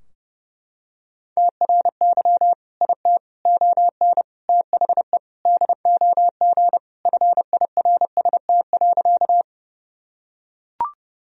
Morse code, heard as a keyed tone, about 30 wpm, 700 Hz. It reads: TRY IT ON THE DOG FIRST.
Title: Sentences from Top 1000 Words - Review Part 2 - 30wpm